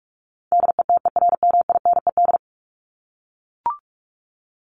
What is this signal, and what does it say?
Morse code, heard as a keyed tone, about 44 wpm, 700 Hz. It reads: DETERMINED
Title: ED Suffix - Top 100 Words with Base Words 45wpm